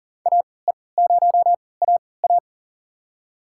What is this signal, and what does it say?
Morse code, heard as a keyed tone, about 40 wpm, spaced as for 23 wpm, 700 Hz. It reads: AE0AA
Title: USA Call Signs - 5 Characters - 3x Character Spacing - 40wpm